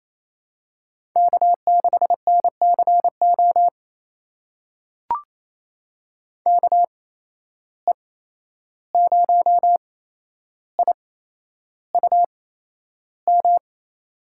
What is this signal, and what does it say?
Morse code, heard as a keyed tone, about 27 wpm, spaced as for 11 wpm, 700 Hz. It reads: K6NCO KE0IUM
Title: USA Call Signs - 8x Character Spacing - 28wpm